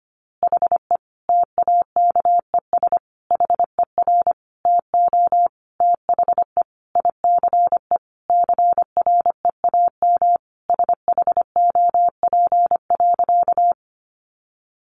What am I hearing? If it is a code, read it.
Morse code, heard as a keyed tone, about 25 wpm, 700 Hz. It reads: HE TAKES HER TO THE ICE CREAM SHOP.
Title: Easy Phrases 25wpm